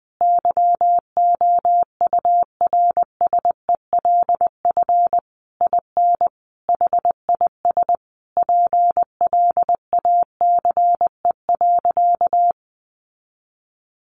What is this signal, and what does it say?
Morse code, heard as a keyed tone, about 20 wpm, 700 Hz. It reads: YOURSELF IN HIS PLACE.
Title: Sentences from Top 1400 Words 20wpm